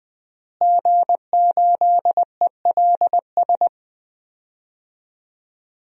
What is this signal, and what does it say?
Morse code, heard as a keyed tone, about 20 wpm, 700 Hz. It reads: G8ELS
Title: DXCC-100 Call Signs - 5 Characters - Repeated 2x - 20wpm